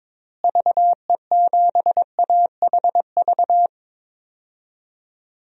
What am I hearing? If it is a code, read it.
Morse code, heard as a keyed tone, about 22 wpm, 700 Hz. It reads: VE7AHV
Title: DXCC-100 Call Signs - 6 Characters - Repeated 2x - 22wpm